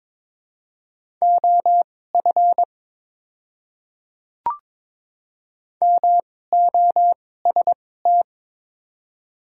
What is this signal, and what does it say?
Morse code, heard as a keyed tone, about 22 wpm, spaced as for 16 wpm, 700 Hz. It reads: OF MOST
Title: 2x Character Spacing - Top 100 Words - 22wpm